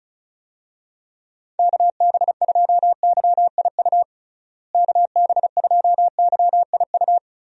Morse code, heard as a keyed tone, about 36 wpm, 700 Hz. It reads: KB2YIU KB2YIU
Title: USA Call Signs - 6 Characters - Repeated 3x - 35wpm